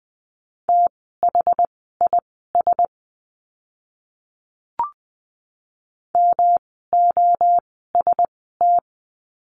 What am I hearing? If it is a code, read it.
Morse code, heard as a keyed tone, about 20 wpm, spaced as for 14 wpm, 700 Hz. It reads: THIS MOST
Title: 139 Lesson 23f - Letter f - Words 20wpm